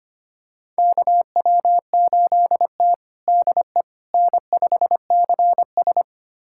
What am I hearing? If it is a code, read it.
Morse code, heard as a keyed tone, about 25 wpm, 700 Hz. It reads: KW8T DE N5CS
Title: USA Callsign DE Callsign - 4 Characters - 25wpm